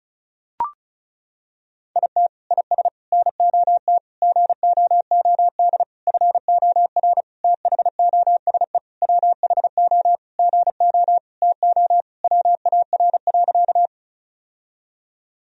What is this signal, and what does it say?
Morse code, heard as a keyed tone, about 36 wpm, 700 Hz. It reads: IT IS NOT GOOD FOR THOSE WHO GO TO WAR.
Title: Sentences from Top 1000 Words - Review Part 2 - 35wpm